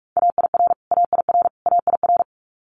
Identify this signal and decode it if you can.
Morse code, heard as a keyed tone, about 44 wpm, 700 Hz. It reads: AIR AIR AIR
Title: Top 200 Words Repeated 45wpm